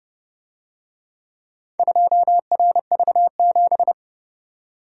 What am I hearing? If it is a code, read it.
Morse code, heard as a keyed tone, about 31 wpm, 700 Hz. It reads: 2RV7